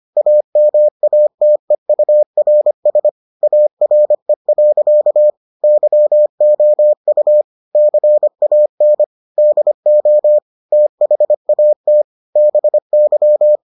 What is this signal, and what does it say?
Morse code, heard as a keyed tone, about 25 wpm, 600 Hz. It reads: AMATEURS ARE. YOU CAN DO THAT BY